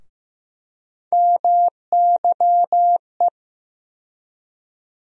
Morse code, heard as a keyed tone, about 15 wpm, 700 Hz. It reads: MYE
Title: USA Call Sign Postfixes - 3 Letters 15wpm